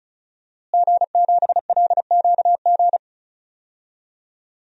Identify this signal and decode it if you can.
Morse code, heard as a keyed tone, about 34 wpm, 700 Hz. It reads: G7LQG
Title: DXCC-100 Call Signs - 5 Characters - Repeated 2x - 35wpm